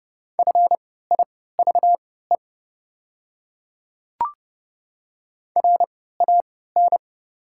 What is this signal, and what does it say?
Morse code, heard as a keyed tone, about 31 wpm, spaced as for 17 wpm, 700 Hz. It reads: FIVE RAN
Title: 3x Character Spacing - Top 500 Words - 30wpm